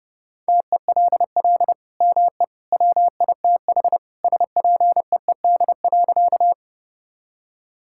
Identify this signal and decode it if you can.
Morse code, heard as a keyed tone, about 30 wpm, 700 Hz. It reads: TELL ME WITH SPEED.
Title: Sentences from Top 1000 Words - Review Part 5 - 30wpm